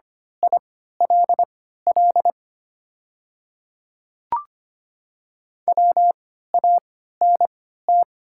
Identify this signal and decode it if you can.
Morse code, heard as a keyed tone, about 25 wpm, spaced as for 14 wpm, 700 Hz. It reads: ILL WANT
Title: 3x Character Spacing - Top 300 Words - 25wpm